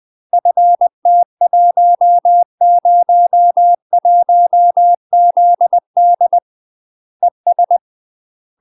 Morse code, heard as a keyed tone, about 20 wpm, 700 Hz. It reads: FT101ZD ES
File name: UK MASTER QSOs TRAINING/20WPM